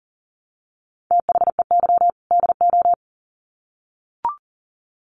Morse code, heard as a keyed tone, about 40 wpm, 700 Hz. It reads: THEY DO
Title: Sets of 2 Words 40wpm